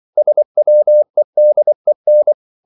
Morse code, heard as a keyed tone, about 24 wpm, 600 Hz. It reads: SWEDEN
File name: CW / Practice / NationalCapitals24